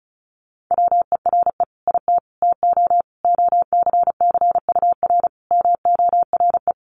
Morse code, heard as a keyed tone, about 34 wpm, 700 Hz. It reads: WERE IT TO OCCUR MORE
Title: Sentences from Top 1300 Words 35wpm